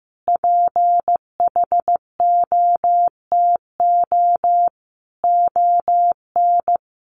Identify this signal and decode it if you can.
Morse code, heard as a keyed tone, about 15 wpm, 700 Hz. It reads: PHOTO ON
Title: Sentences from Top 1200 Words 15wpm